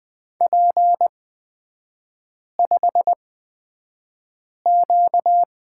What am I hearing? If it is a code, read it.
Morse code, heard as a keyed tone, about 20 wpm, spaced as for 5 wpm, 690 Hz. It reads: P5Q
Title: 255 Lesson 41e - Period - Three Characters 20wpm